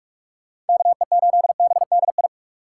Morse code, heard as a keyed tone, about 59 wpm, spaced as for 52 wpm, 700 Hz. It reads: KE8BDI
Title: Speed Racing - USA Call Signs 45wpm